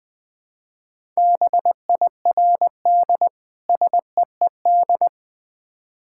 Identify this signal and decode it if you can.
Morse code, heard as a keyed tone, about 20 wpm, 690 Hz. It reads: BIRD SEED
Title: Difficult Words with Hints - 20wpm